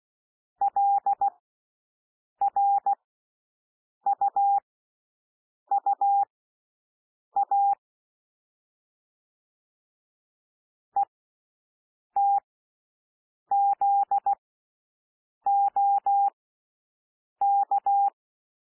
Morse code, heard as a keyed tone, about 16 wpm, spaced as for 6 wpm, 800 Hz. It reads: LRUUA ETZOK
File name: cwmp3 / L1 SOUND LETTERS 16 wpm